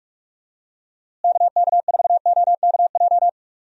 Morse code, heard as a keyed tone, about 44 wpm, 700 Hz. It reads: KK4OKJ
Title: USA Call Signs - 6 Characters - Repeated 2x - 45wpm